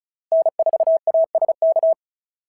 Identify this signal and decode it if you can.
Morse code, heard as a keyed tone, about 36 wpm, 680 Hz. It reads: N4ASK
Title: SOTA QSO Practice 35wpm